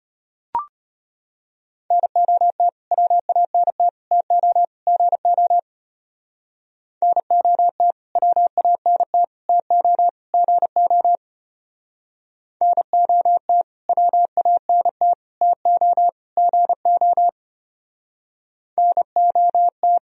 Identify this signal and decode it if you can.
Morse code, heard as a keyed tone, about 31 wpm, 700 Hz. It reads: NOT WANT TO GO NOT WANT TO GO NOT WANT TO GO NOT